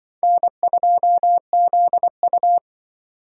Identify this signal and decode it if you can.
Morse code, heard as a keyed tone, about 23 wpm, 700 Hz. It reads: N2ZU
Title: Speed Racing - USA 1x2 Call Signs 20wpm